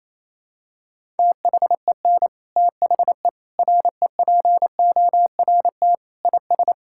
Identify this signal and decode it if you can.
Morse code, heard as a keyed tone, about 27 wpm, 700 Hz. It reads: THEN THE REPORT IS
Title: Sentences from Top 1000 Words - Review Part 6 - 28wpm